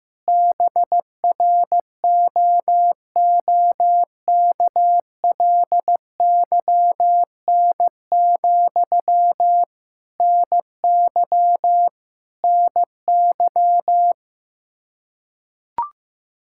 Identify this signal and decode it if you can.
Morse code, heard as a keyed tone, about 15 wpm, 700 Hz. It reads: BROOKLYN, NY NY
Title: Practice QSO - Exchange 1 - Beginning - 15wpm